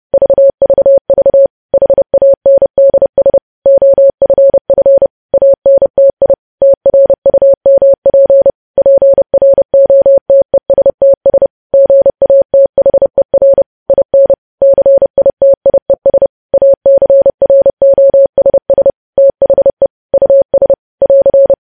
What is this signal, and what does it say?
Morse code, heard as a keyed tone, about 30 wpm, 550 Hz. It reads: VVV HANDS OFF ANTI TRUMP PROTESTS GATHER IN CITIES ACROSS THE US <AR>
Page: news headlines in morse code at 30 wpm